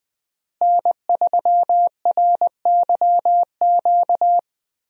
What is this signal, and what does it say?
Morse code, heard as a keyed tone, about 20 wpm, 700 Hz. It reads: N3RYQ
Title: USA Call Signs - 5 Characters - Repeated 2x - 20wpm